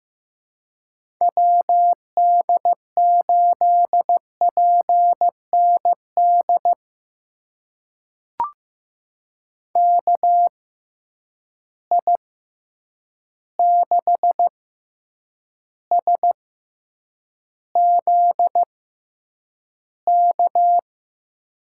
Morse code, heard as a keyed tone, about 15 wpm, spaced as for 8 wpm, 700 Hz. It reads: WD8PND KI6SZK
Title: USA Call Signs - 6 Characters - 6x Character Spacing - 15wpm